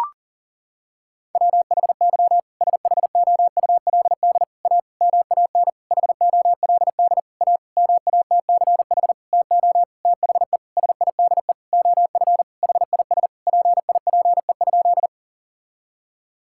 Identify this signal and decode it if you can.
Morse code, heard as a keyed tone, about 40 wpm, 700 Hz. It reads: WHY SHOULD A MAN HOLD A MATCH TO THE SIDE OF HIS PIPE?
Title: Sentences from Top 1600 Words 40wpm